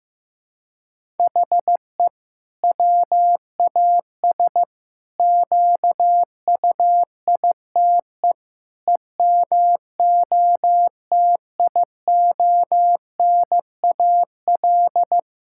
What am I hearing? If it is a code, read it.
Morse code, heard as a keyed tone, about 15 wpm, 700 Hz. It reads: HE WAS QUITE EMOTIONAL
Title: Sentences from Top 1000 Words - Review Part 5 - 15wpm